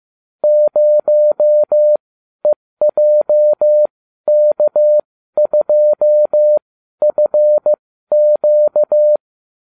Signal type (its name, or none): Morse code